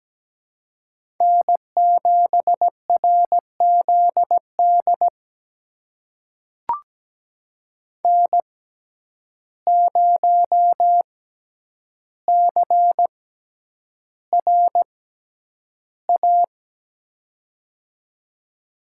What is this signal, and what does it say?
Morse code, heard as a keyed tone, about 17 wpm, spaced as for 9 wpm, 700 Hz. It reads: N7RZD N0CRA
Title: USA Call Signs - 5 Characters - 6x Character Spacing - 17wpm